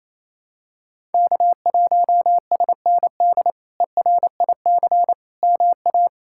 Morse code, heard as a keyed tone, about 28 wpm, 700 Hz. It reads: K1SND ERIC MA
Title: K1USN SST QSO Practice - 28wpm